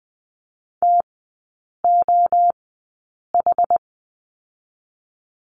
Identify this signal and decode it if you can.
Morse code, heard as a keyed tone, about 20 wpm, spaced as for 8 wpm, 700 Hz. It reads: TOH
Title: 074 Lesson 13e - Letter L - Three Characters 20wpm